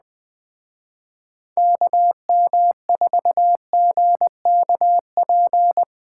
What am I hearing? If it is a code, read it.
Morse code, heard as a keyed tone, about 21 wpm, 700 Hz. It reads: KM4GKP
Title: Speed Racing - USA Call Signs 17wpm